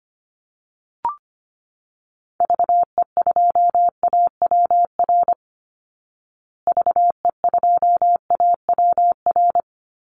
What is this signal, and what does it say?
Morse code, heard as a keyed tone, about 25 wpm, 700 Hz. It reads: VE2AWR VE2AWR
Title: Call Signs Repeated 3x - 25wpm